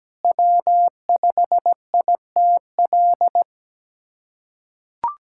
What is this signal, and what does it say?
Morse code, heard as a keyed tone, about 17 wpm, 700 Hz. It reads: W5ITL
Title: USA Callsign DE Callsign - 5 Characters - 17wpm